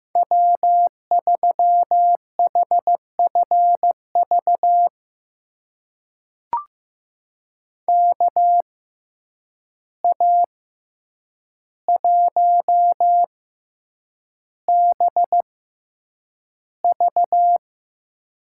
Morse code, heard as a keyed tone, about 15 wpm, spaced as for 8 wpm, 700 Hz. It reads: W3HFV KA1BV
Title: USA Call Signs - 5 Characters - 6x Character Spacing - 15wpm